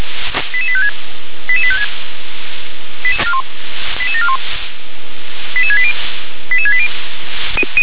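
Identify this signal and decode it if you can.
Five-tone selective call (ZVEI), heard as quick runs of five tones, tones of about 70 ms. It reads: 88856 88856 88741 88680 88680
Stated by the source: ZVEI five-tone selective call received on SDR.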